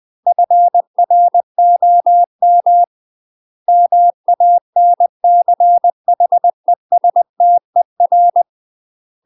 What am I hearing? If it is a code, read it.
Morse code, heard as a keyed tone, about 20 wpm, 700 Hz. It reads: FROM MANCHESTER
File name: UK MASTER QSOs TRAINING / 20WPM / UK MASTER QSOs 0064